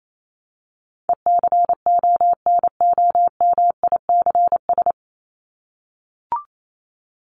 Morse code, heard as a keyed tone, about 28 wpm, 700 Hz. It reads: ECONOMICS
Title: Top 501-750 Words 28wpm